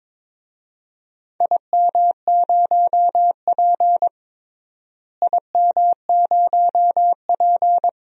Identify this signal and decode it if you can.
Morse code, heard as a keyed tone, about 22 wpm, 700 Hz. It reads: IM0P IM0P
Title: DXCC-100 Call Signs - 4 Characters - Repeated 3x - 22wpm